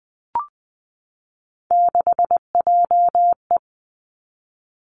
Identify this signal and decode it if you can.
Morse code, heard as a keyed tone, about 20 wpm, 700 Hz. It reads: 6JE
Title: Three Challenging Letter-Number Combinations 20wpm